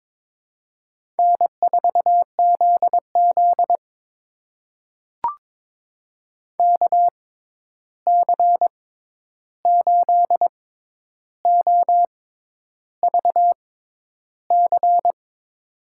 Morse code, heard as a keyed tone, about 22 wpm, spaced as for 10 wpm, 700 Hz. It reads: N4ZZ KC8OVC
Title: USA Call Signs - 6x Character Spacing - 22wpm